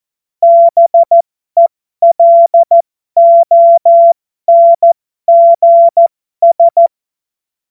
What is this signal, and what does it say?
Morse code, heard as a keyed tone, about 14 wpm, spaced as for 12 wpm, 680 Hz. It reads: BELONGS
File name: Morsekurs / Witze